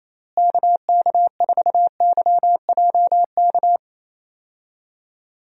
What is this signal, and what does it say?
Morse code, heard as a keyed tone, about 28 wpm, 700 Hz. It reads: KK4YJK
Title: USA Call Signs - 6 Characters - Repeated 2x - 28wpm